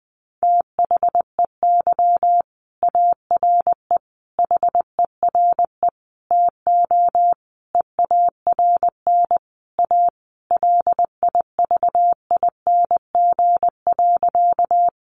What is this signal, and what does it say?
Morse code, heard as a keyed tone, about 20 wpm, 700 Hz. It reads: THEY ARE HERE TO EARN A LIVING.
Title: Sentences from Top 1500 Words 20wpm